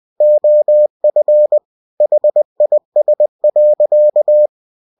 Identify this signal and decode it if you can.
Morse code, heard as a keyed tone, about 20 wpm, 600 Hz. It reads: OF HIS.
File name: cw-books / winnie-the-pooh / winnie-20wpm-page-0024